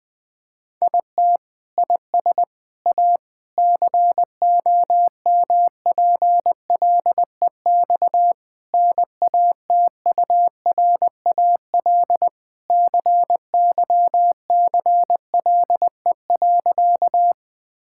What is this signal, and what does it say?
Morse code, heard as a keyed tone, about 20 wpm, 700 Hz. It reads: IT IS A COMPLEX NATURAL CYCLE.
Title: Sentences from Top 1000 Words - Review Part 6 - 20wpm